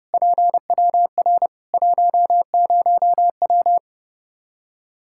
Morse code, heard as a keyed tone, about 30 wpm, 700 Hz. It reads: PWR 10W
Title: Practice QSO - Exchange 2 - Info - 30wpm